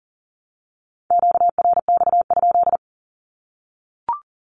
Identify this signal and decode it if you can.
Morse code, heard as a keyed tone, about 40 wpm, 700 Hz. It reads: QRX?
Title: Q-Codes 40wpm